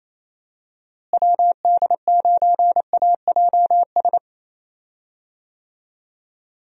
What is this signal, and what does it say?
Morse code, heard as a keyed tone, about 28 wpm, 700 Hz. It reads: WD9AJS
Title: USA Call Signs - 6 Characters - Repeated 3x - 28wpm